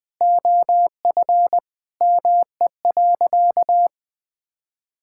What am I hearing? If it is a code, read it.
Morse code, heard as a keyed tone, about 20 wpm, 700 Hz. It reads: OF ME.
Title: Sentences with 2-Word Phrases - Spoken Beginning - 20wpm